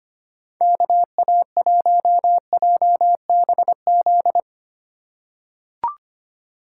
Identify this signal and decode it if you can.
Morse code, heard as a keyed tone, about 25 wpm, 700 Hz. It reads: KA1JBZ
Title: USA Call Signs - 2x Character Spacing - 25wpm